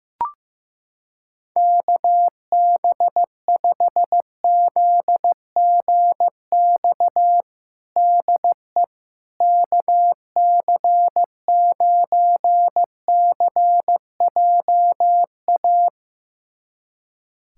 Morse code, heard as a keyed tone, about 16 wpm, 700 Hz. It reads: KB5ZGX DE KC9CJA
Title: USA Callsign DE Callsign - 6 Characters - 15wpm